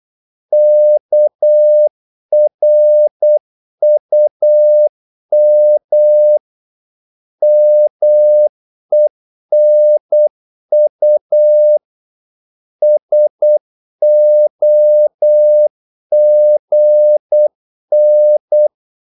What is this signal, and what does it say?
Morse code, heard as a keyed tone, about 8 wpm, 600 Hz. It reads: KRUM MENU SOGN
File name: Korte ord DK 8wpm